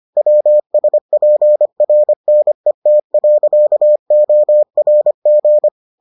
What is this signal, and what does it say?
Morse code, heard as a keyed tone, about 25 wpm, 600 Hz. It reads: WSPRNET.ORG